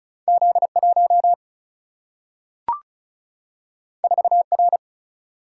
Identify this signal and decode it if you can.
Morse code, heard as a keyed tone, about 36 wpm, 700 Hz. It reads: Z1 4R
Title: Single-Letter Single-Number 35wpm